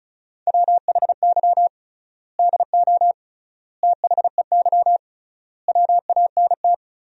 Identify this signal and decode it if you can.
Morse code, heard as a keyed tone, about 36 wpm, 700 Hz. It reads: WHY DO THEY WANT
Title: 3x Word-Spacing - Sentences from Top 400 Words - 35wpm